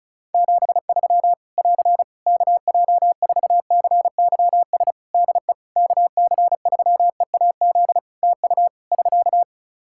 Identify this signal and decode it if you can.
Morse code, heard as a keyed tone, about 36 wpm, 700 Hz. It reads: 73 <AR> KJ4CYS DE KC3EAZ TU <SK>
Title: Practice QSO - Exchange 4 - Ending - 35wpm